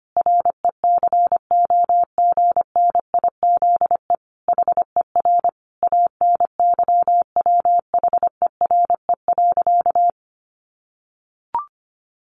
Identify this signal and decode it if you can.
Morse code, heard as a keyed tone, about 25 wpm, 700 Hz. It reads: RECOGNIZE HER ANYWHERE.
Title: Sentences from Top 1400 Words 25wpm